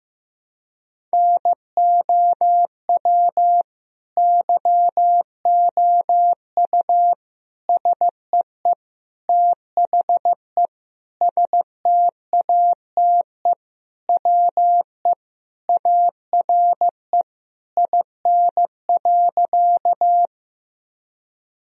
Morse code, heard as a keyed tone, about 15 wpm, 700 Hz. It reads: NOW YOU SEE THE STATE WE ARE IN.
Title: Sentences from Top 1000 Words - Review Part 2 - 15wpm